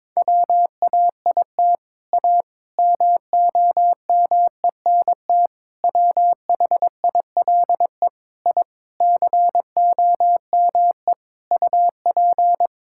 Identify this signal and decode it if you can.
Morse code, heard as a keyed tone, about 22 wpm, 700 Hz. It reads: WAIT A MOMENT WHILE I COME UP
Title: Sentences from Top 1000 Words - Review Part 5 - 22wpm